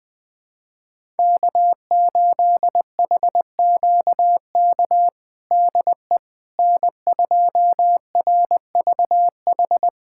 Morse code, heard as a keyed tone, about 20 wpm, 700 Hz. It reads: K8HQK DE N2RVH
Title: USA Callsign DE Callsign - 5 Characters - 20wpm